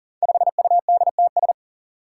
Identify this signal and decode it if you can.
Morse code, heard as a keyed tone, about 41 wpm, 700 Hz. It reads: 5UDTS
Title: Five Challenging Letter-Number Combinations 40wpm